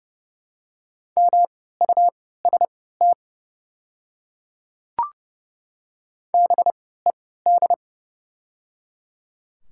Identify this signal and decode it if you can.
Morse code, heard as a keyed tone, about 30 wpm, spaced as for 17 wpm, 700 Hz. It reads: MUST BED
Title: 3x Character Spacing - Top 500 Words - 30wpm